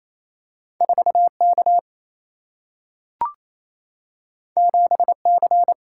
Morse code, heard as a keyed tone, about 28 wpm, 700 Hz. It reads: 4K 7C
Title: Single-Letter Single-Number 28wpm